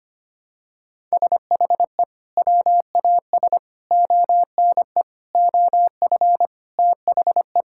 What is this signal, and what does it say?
Morse code, heard as a keyed tone, about 25 wpm, 700 Hz. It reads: SHE WAS ONE OF THE
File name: practice / 25wpm / Sentences with 4-Word Phrases - 25wpm